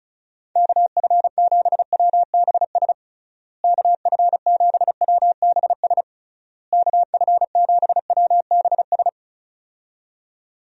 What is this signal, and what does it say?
Morse code, heard as a keyed tone, about 36 wpm, 700 Hz. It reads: KF7WBS KF7WBS KF7WBS
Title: USA Call Signs - 6 Characters - Repeated 3x - 35wpm